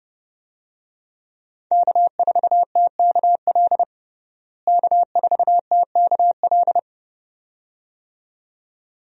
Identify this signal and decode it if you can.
Morse code, heard as a keyed tone, about 30 wpm, 700 Hz. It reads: K4TKL K4TKL
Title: USA Call Signs - 5 Characters - Repeated 2x - 30wpm